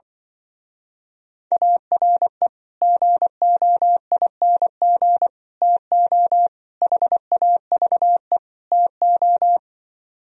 Morse code, heard as a keyed tone, about 25 wpm, 700 Hz. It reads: ARE GOING TO HAVE TO
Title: Speed-Racing - Sets of 5 Words 20wpm